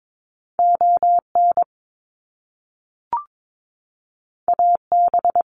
Morse code, heard as a keyed tone, about 22 wpm, 700 Hz. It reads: ON AB
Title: Canadian Provinces and Territories - Repeated 3x - 22wpm